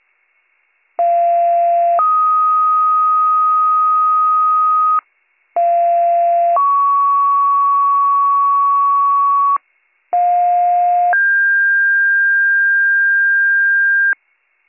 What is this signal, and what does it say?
Two-tone page, heard as a one-second tone, then a three-second tone, about 690 then 1250 Hz, about 690 then 1120 Hz, about 710 then 1600 Hz. It is heard from 1.0 to 14.1 s.